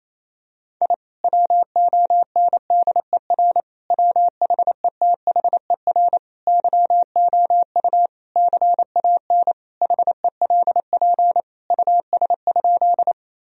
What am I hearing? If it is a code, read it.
Morse code, heard as a keyed tone, about 28 wpm, 700 Hz. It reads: I WONDER WHETHER YOU CAN HELP US?
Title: Sentences from Top 1000 Words - Review Part 3 - 28wpm